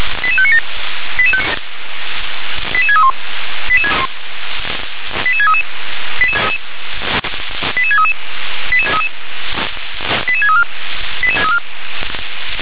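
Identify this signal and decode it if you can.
Five-tone selective call (ZVEI), heard as quick runs of five tones, tones of about 70 ms. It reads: 88597 88631 88633 88633 88634
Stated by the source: ZVEI five-tone selective call 88716 received with an SDR.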